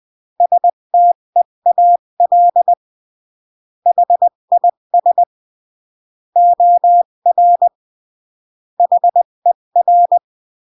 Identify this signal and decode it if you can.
Morse code, heard as a keyed tone, about 20 wpm, 700 Hz. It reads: STEAL HIS OR HER
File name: CW / jodorowsky / 82 commandments 20-15